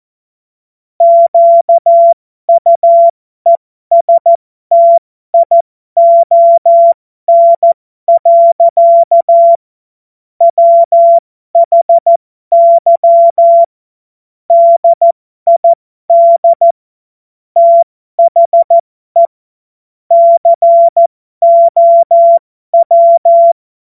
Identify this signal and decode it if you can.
Morse code, heard as a keyed tone, about 14 wpm, spaced as for 12 wpm, 680 Hz. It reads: QUESTION. WHY DID THE COW
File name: Morsekurs / Witze